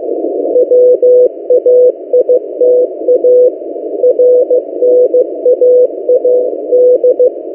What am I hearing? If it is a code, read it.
Morse code, heard as a keyed tone, about 15 wpm, 500 Hz. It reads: WAITA RNAAD
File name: M08a- -1355utc-8096khz